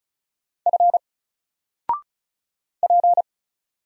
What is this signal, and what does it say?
Morse code, heard as a keyed tone, about 34 wpm, 700 Hz. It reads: F P
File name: Single-Letter-Number-35wpm